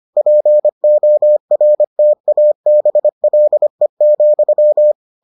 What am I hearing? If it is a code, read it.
Morse code, heard as a keyed tone, about 25 wpm, 600 Hz. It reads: PORTABLE,